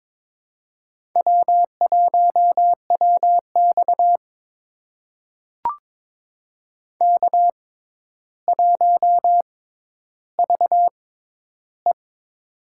Morse code, heard as a keyed tone, about 22 wpm, spaced as for 11 wpm, 700 Hz. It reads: W1WX K1VE
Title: USA Call Signs - 6x Character Spacing - 22wpm